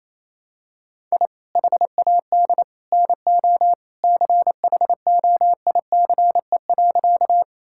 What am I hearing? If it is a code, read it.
Morse code, heard as a keyed tone, about 28 wpm, 700 Hz. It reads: I HAD NO CHOICE.
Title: Sentences from Top 1000 Words - Review Part 6 - 28wpm